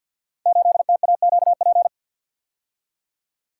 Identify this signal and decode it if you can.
Morse code, heard as a keyed tone, about 48 wpm, 700 Hz. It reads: 8TAQP
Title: Code Groups - 5 Letter:Numbers - 50wpm